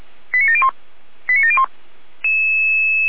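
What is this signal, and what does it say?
Five-tone selective call (ZVEI), heard as quick runs of five tones, tones of about 70 ms. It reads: 89701 89701
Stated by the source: ZVEI five-tone selective call received by SDR.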